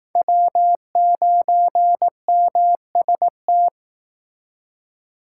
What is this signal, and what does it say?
Morse code, heard as a keyed tone, about 18 wpm, 700 Hz. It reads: W9MST
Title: Speed Racing - USA 1x3 Call Signs 15wpm